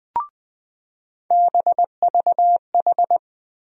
Morse code, heard as a keyed tone, about 20 wpm, 700 Hz. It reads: BVH
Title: USA Call Sign Postfixes - 3 Letters 20wpm